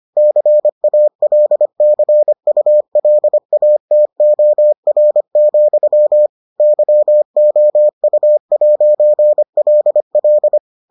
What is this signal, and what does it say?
Morse code, heard as a keyed tone, about 25 wpm, 600 Hz. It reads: CALCULATOR, YOU'LL